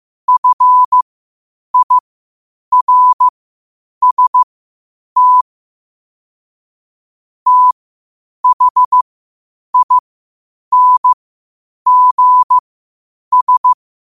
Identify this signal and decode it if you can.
Morse code, heard as a keyed tone, about 15 wpm, spaced as for 8 wpm, 1000 Hz. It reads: FIRST THINGS
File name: 8wpm